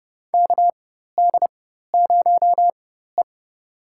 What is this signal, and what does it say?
Morse code, heard as a keyed tone, about 31 wpm, spaced as for 14 wpm, 700 Hz. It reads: KD0E
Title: USA Call Signs - 4 Characters - 4x Character Spacing - 30wpm